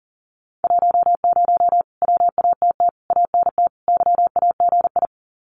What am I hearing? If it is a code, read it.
Morse code, heard as a keyed tone, about 38 wpm, 700 Hz. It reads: 10 WATT ANT YAGI
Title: Common QSO Phrases Encore 40wpm